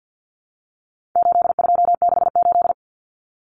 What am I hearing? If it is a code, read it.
Morse code, heard as a keyed tone, about 50 wpm, 700 Hz. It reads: 8268
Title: Numbers - 4-Digit 50wpm